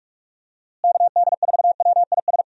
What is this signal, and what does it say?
Morse code, heard as a keyed tone, about 46 wpm, 700 Hz. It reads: KD4WIS
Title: USA Call Signs - 6 Characters - Repeated 2x - 45wpm